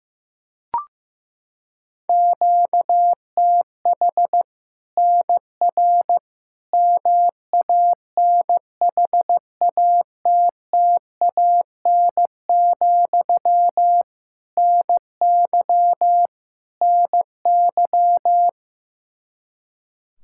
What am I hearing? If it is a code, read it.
Morse code, heard as a keyed tone, about 16 wpm, 700 Hz. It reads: QTH NR MANHATTAN, NY NY
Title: Practice QSO - Exchange 1 - Beginning - 15wpm